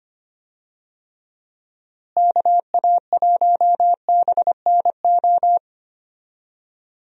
Morse code, heard as a keyed tone, about 25 wpm, 700 Hz. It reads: KA1BNO
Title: USA Call Signs - 6 Characters - Repeated 3x - 25wpm